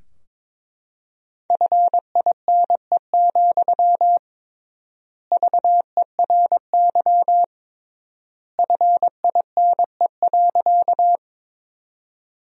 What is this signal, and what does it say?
Morse code, heard as a keyed tone, about 21 wpm, 700 Hz. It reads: FINE, VERY FINE.
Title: 3x Word-Spacing - Sentences from Top 500 Words - 22wpm